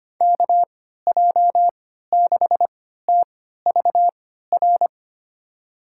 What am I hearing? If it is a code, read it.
Morse code, heard as a keyed tone, about 25 wpm, spaced as for 14 wpm, 700 Hz. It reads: KJ6TVR
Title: USA Call Signs - 6 Characters - 3x Character Spacing - 25wpm